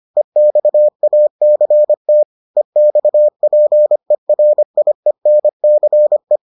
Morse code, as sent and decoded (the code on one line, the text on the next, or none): . -..- .- -.-. - / . -..- .--. .-. .. -. -.-. .
EXACT EXPERIENCE